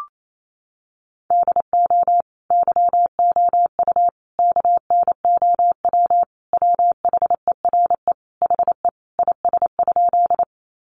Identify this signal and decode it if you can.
Morse code, heard as a keyed tone, about 28 wpm, 700 Hz. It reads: DO YOU KNOW WHERE HE IS?
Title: Sentences from Top 300 Words 28wpm